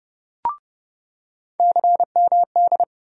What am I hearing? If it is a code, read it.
Morse code, heard as a keyed tone, about 30 wpm, 700 Hz. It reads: CMD
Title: USA Call Sign Postfixes - 3 Letters 30wpm